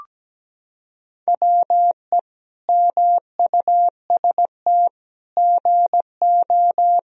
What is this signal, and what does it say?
Morse code, heard as a keyed tone, about 17 wpm, 700 Hz. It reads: WE MUST GO
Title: Sentences from Top 1000 Words - Review Part 3 - 17wpm